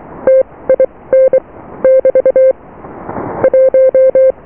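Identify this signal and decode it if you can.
Morse code, heard as a keyed tone, about 23 wpm, spaced as for 18 wpm, 520 Hz. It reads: TIN <BT> 1
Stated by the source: Morse QSO on 14 MHz with a German station, received at RST 579.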